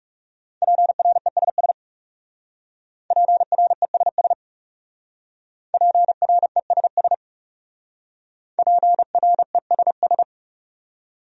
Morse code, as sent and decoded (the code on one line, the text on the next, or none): .--. .-. ... ... / .--. .-. ... ... / .--. .-. ... ... / .--. .-. ... ...
PRESS PRESS PRESS PRESS